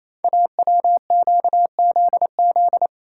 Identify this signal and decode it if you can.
Morse code, heard as a keyed tone, about 29 wpm, 700 Hz. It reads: AWQZZ
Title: Code Groups - 5 Letters - 28wpm